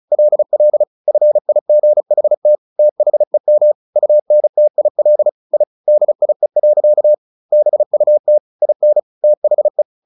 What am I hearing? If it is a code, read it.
Morse code, heard as a keyed tone, about 34 wpm, 600 Hz. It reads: LL FIGHT THEM UNTIL I DIE. BUT IN THE
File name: the old man and the sea 2 35wpm